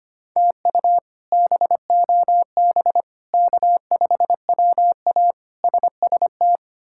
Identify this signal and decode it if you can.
Morse code, heard as a keyed tone, about 25 wpm, 700 Hz. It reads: TU BOB K5WA SST
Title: K1USN SST QSO Practice - 25wpm